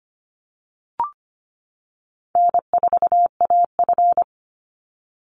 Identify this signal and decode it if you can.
Morse code, heard as a keyed tone, about 25 wpm, 700 Hz. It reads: N4AF
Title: Easy Call Signs - 25wpm